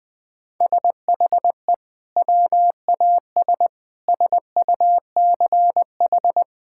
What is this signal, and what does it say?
Morse code, heard as a keyed tone, about 20 wpm, 700 Hz. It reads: SHE WAS SUCH
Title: Sentences from Top 1600 Words 20wpm